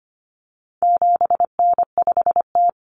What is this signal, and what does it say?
Morse code, heard as a keyed tone, about 25 wpm, 700 Hz. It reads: 7N5T
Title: Common Cut Numbers - 4-Digit 25wpm